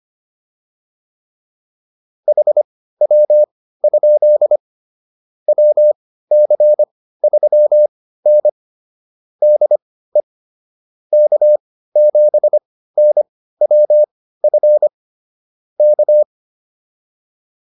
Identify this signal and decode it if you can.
Morse code, heard as a keyed tone, about 25 wpm, spaced as for 15 wpm, 600 Hz. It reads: HW? WC3N DE K7NWF K